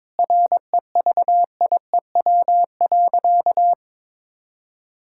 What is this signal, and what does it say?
Morse code, heard as a keyed tone, about 21 wpm, 700 Hz. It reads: REVIEW.
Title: Sentences from Top 1000 Words - Review Part 6 - 22wpm